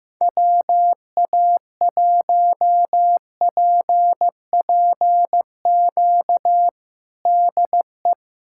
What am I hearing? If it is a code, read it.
Morse code, heard as a keyed tone, about 15 wpm, 700 Hz. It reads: WA1PPQ DE
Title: USA Callsign DE Callsign - 6 Characters - 15wpm